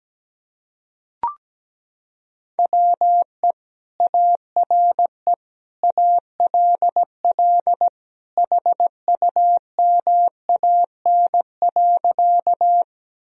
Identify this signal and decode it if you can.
Morse code, heard as a keyed tone, about 17 wpm, 700 Hz. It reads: WE ARE ALL HUMAN.
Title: Sentences from Top 1000 Words - Review Part 3 - 17wpm